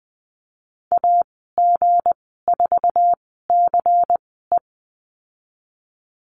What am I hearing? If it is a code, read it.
Morse code, heard as a keyed tone, about 20 wpm, spaced as for 15 wpm, 700 Hz. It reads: AG4CE
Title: 173 Lesson 28g - Number 9 - Call Signs 20wpm